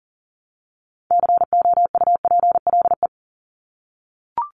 Morse code, heard as a keyed tone, about 40 wpm, 700 Hz. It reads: COUPLE
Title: Top 1001-1250 Words 40wpm